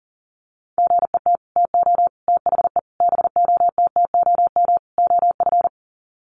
Morse code, heard as a keyed tone, about 40 wpm, 700 Hz. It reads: GET TO THE BOTTOM OF
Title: Sets of 5 Words Encore 40wpm